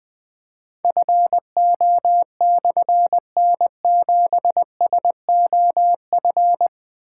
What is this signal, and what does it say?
Morse code, heard as a keyed tone, about 20 wpm, 700 Hz. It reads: FO/N7SOF
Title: Difficult Call Signs 20wpm